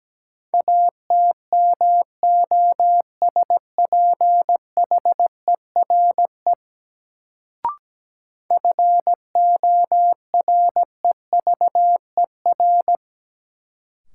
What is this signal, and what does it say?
Morse code, heard as a keyed tone, about 17 wpm, 700 Hz. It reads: ATMOSPHERE FOREVER
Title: Sentences from Top 1600 Words - New Word Drill Alt - 17wpm